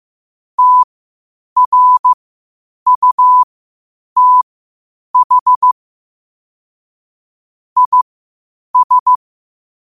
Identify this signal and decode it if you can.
Morse code, heard as a keyed tone, about 15 wpm, spaced as for 8 wpm, 1000 Hz. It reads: TRUTH IS
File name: qotdfiles / 8wpm